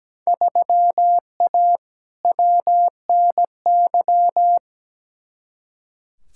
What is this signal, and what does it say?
Morse code, heard as a keyed tone, about 17 wpm, 700 Hz. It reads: 3A WNY
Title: Field Day Exchanges - 17wpm